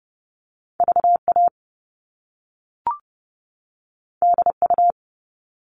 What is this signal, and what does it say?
Morse code, heard as a keyed tone, about 30 wpm, 700 Hz. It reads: VA DU
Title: All Cut Numbers - 2-Digit 30wpm